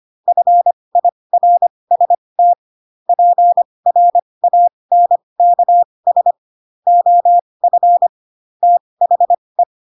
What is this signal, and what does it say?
Morse code, heard as a keyed tone, about 25 wpm, spaced as for 20 wpm, 700 Hz. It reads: FIRST PRANKS OF THE
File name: Pinocchio / Pinocchio ch3 15 25 ik1hge